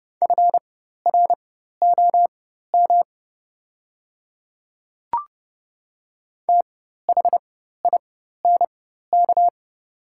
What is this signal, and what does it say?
Morse code, heard as a keyed tone, about 30 wpm, spaced as for 14 wpm, 700 Hz. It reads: FROM THINK
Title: 4x Character Spacing - Top 200 Words - 30wpm